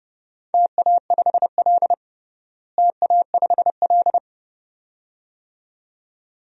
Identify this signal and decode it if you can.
Morse code, heard as a keyed tone, about 30 wpm, 700 Hz. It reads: TA5L TA5L
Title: DXCC-100 Call Signs - 4 Characters - Repeated 2x - 30wpm